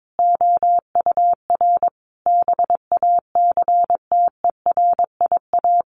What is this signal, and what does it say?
Morse code, heard as a keyed tone, about 22 wpm, 700 Hz. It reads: OUR BACTERIA
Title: A Sound Of Thunder 22wpm